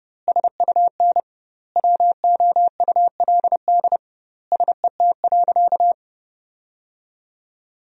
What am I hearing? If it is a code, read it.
Morse code, heard as a keyed tone, about 30 wpm, 700 Hz. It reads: SUN WOULD SET.
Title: 2x Word-Spacing - Sentences from Top 400 Words - 30wpm